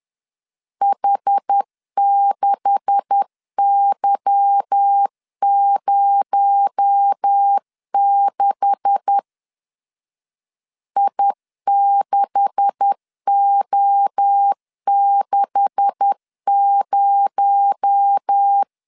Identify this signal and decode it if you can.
Morse code, heard as a keyed tone, about 10 wpm, 800 Hz. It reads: H6Y06 I6O60
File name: LEZIONI / 9 286 10wpm 2735